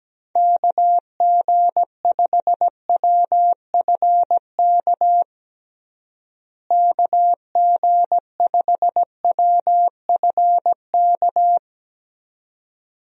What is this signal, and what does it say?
Morse code, heard as a keyed tone, about 17 wpm, 700 Hz. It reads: KG5WFK KG5WFK
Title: USA Call Signs - 6 Characters - Repeated 2x - 17wpm